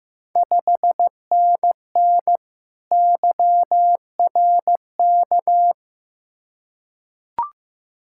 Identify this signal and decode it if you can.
Morse code, heard as a keyed tone, about 15 wpm, 700 Hz. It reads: 5NN YRK
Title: Ontario QSO Party Exchange 15wpm